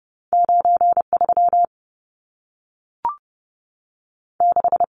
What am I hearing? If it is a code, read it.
Morse code, heard as a keyed tone, about 30 wpm, 700 Hz. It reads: 93 6
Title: Numbers - 2-Digit 30wpm